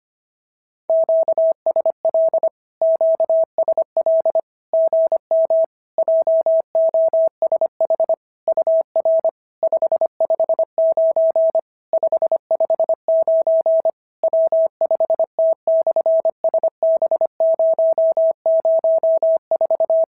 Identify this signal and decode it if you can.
Morse code, heard as a keyed tone, about 25 wpm, 650 Hz. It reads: QSL QSL GM JOSH UR 559 559 W5T/SB004
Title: SOTA QSO Practice 25wpm